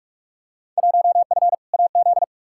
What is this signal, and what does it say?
Morse code, heard as a keyed tone, about 41 wpm, 700 Hz. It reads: 1F AZ
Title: Field Day Exchanges - 45wpm